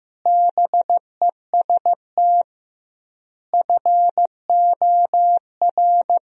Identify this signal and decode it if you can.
Morse code, heard as a keyed tone, about 15 wpm, 700 Hz. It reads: BEST FOR
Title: 2x Word-Spacing - Sentences from Top 500 Words - 15wpm